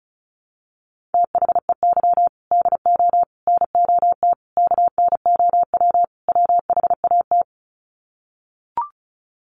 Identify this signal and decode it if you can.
Morse code, heard as a keyed tone, about 34 wpm, 700 Hz. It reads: THEY DO NOT KNOW WHAT
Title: Sets of 5 Words Encore 35wpm